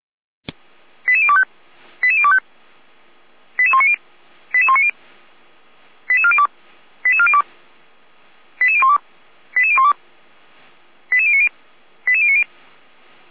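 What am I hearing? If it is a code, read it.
Five-tone selective call (ZVEI), heard as quick runs of five tones, tones of about 70 ms. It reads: 88025 88025 88190 88190 88402 88402 88012 88012 88090 88090
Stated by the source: ZVEI five-tone selective call 88102 received by SDR, archived on the Feuerwehr Orlamünde fire brigade server.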